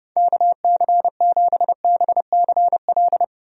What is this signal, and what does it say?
Morse code, heard as a keyed tone, about 32 wpm, 700 Hz. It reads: KC7BCL
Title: USA Call Signs - 6 Characters - 30wpm